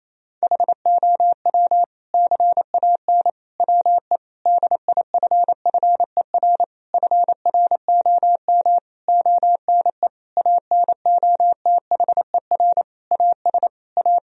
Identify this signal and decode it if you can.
Morse code, heard as a keyed tone, about 28 wpm, 700 Hz. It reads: HOW CAN WE DIFFER FROM ONE ANOTHER AS A
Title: Sentences from Top 1000 Words - Review Part 1 - 28wpm